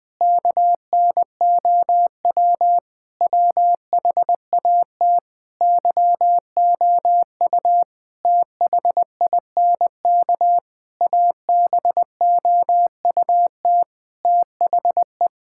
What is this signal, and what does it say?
Morse code, heard as a keyed tone, about 21 wpm, 700 Hz. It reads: KNOW WHAT YOU THINK ABOUT THE
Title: Sentences with 6-Word Phrases - 20wpm